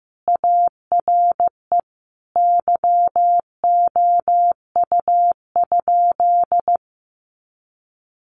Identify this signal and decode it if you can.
Morse code, heard as a keyed tone, about 15 wpm, 700 Hz. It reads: ARE YOU?
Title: Sentences from Top 100 Words 15wpm